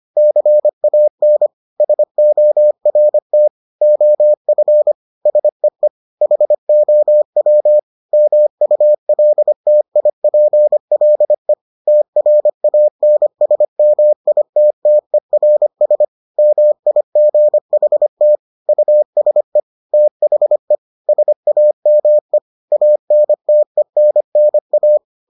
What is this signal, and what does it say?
Morse code, heard as a keyed tone, about 25 wpm, 600 Hz. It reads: CAN SORT OF SEE HOW MULTIPLE TRANSMITTERS MIGHT USE THE SAME ANTENNA